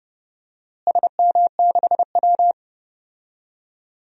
Morse code, heard as a keyed tone, about 32 wpm, 700 Hz. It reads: SM6W
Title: DXCC-100 Call Signs - 4 Characters - Repeated 3x - 30wpm